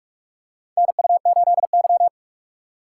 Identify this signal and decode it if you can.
Morse code, heard as a keyed tone, about 46 wpm, 700 Hz. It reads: NU8Y
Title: USA Call Signs - 3x Character Spacing - 45wpm